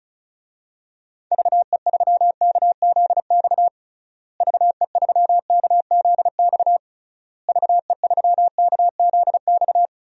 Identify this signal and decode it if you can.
Morse code, heard as a keyed tone, about 36 wpm, 700 Hz. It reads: VE3KZX VE3KZX VE3KZX
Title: DXCC-100 Call Signs - 6 Characters - Repeated 3x - 35wpm